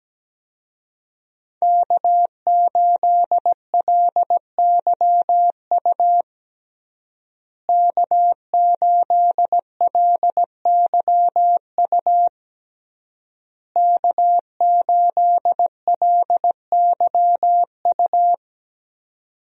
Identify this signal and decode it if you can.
Morse code, heard as a keyed tone, about 17 wpm, 700 Hz. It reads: K8LYU K8LYU K8LYU
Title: USA Call Signs - 5 Characters - Repeated 3x - 17wpm